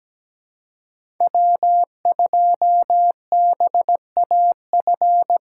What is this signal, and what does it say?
Morse code, heard as a keyed tone, about 17 wpm, 700 Hz. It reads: W2BAF
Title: USA Call Signs - 3x Character Spacing - 17wpm